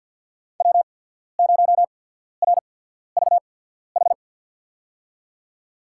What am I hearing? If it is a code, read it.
Morse code, heard as a keyed tone, about 50 wpm, spaced as for 14 wpm, 700 Hz. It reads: W0RVH
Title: USA Call Signs - 5 Characters - 8x Character Spacing - 50wpm